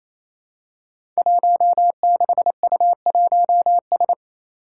Morse code, heard as a keyed tone, about 27 wpm, 700 Hz. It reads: 16U1S
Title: Five Challenging Letter-Number Combinations 28wpm